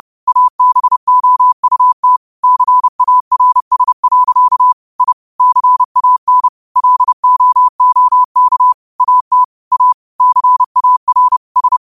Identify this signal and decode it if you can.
Morse code, heard as a keyed tone, about 30 wpm, 1000 Hz. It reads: ABOUT CARS. I CAN LOOK AT A CAR S